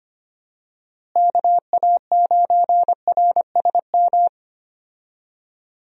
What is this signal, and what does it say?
Morse code, heard as a keyed tone, about 25 wpm, 700 Hz. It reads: KA9RSM
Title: USA Call Signs - 6 Characters - Repeated 3x - 25wpm